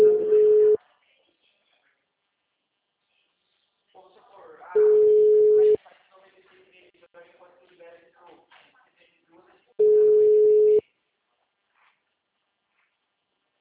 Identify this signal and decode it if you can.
Ringback tone (Europe, 425 Hz), with speech.